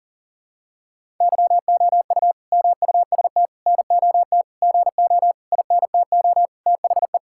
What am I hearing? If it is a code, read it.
Morse code, heard as a keyed tone, about 40 wpm, 700 Hz. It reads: YOU MUST NOT GO INTO THE
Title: Sentences from Top 1000 Words - Review Part 5 - 40wpm